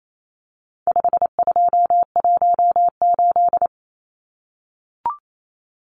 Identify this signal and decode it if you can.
Morse code, heard as a keyed tone, about 28 wpm, 700 Hz. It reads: 5218
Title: Numbers - 4-Digit 28wpm